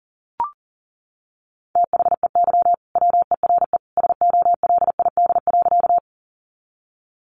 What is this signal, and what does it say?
Morse code, heard as a keyed tone, about 40 wpm, 700 Hz. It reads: THEY WERE SOLID.
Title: Sentences from Top 1500 Words 40wpm